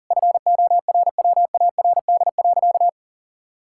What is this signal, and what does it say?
Morse code, heard as a keyed tone, about 37 wpm, 700 Hz. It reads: FORWARD.
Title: Pangram Challenge v2 - 40wpm